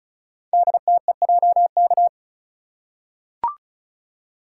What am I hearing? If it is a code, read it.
Morse code, heard as a keyed tone, about 36 wpm, 700 Hz. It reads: DTEJK
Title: Code Groups - 5 Letters - 35wpm